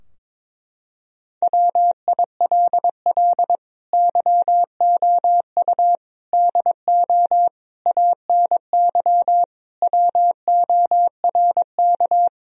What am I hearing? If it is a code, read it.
Morse code, heard as a keyed tone, about 22 wpm, 700 Hz. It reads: WILL YOU DO ANY WORK
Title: Sentences from Top 1000 Words - Review Part 3 - 22wpm